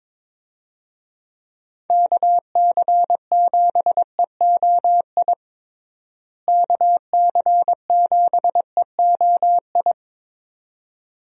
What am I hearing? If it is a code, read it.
Morse code, heard as a keyed tone, about 22 wpm, 700 Hz. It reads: KC7EOI KC7EOI
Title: USA Call Signs - 6 Characters - Repeated 2x - 22wpm